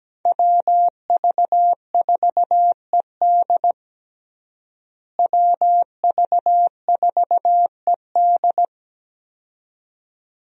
Morse code, heard as a keyed tone, about 17 wpm, 700 Hz. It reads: WV4ED WV4ED
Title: USA Call Signs - 5 Characters - Repeated 2x - 17wpm